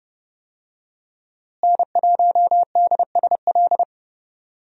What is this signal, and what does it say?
Morse code, heard as a keyed tone, about 32 wpm, 700 Hz. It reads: N1DSL
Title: USA Call Signs - 5 Characters - Repeated 2x - 30wpm